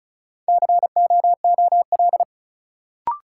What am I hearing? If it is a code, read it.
Morse code, heard as a keyed tone, about 34 wpm, 700 Hz. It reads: COOL
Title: Sentences from Top 1200 Words - New Word Drill - 35wpm